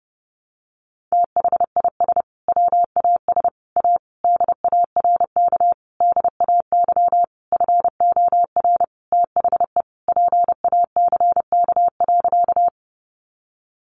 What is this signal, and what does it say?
Morse code, heard as a keyed tone, about 30 wpm, 700 Hz. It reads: THIS WAS A DARK DAY FOR THE PACK.
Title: Sentences from Top 1300 Words 30wpm